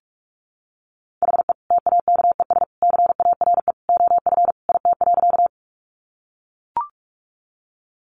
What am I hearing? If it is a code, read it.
Morse code, heard as a keyed tone, about 44 wpm, 700 Hz. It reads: HE TAKES CARE OF IT.
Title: Additional Easy Phrases 45wpm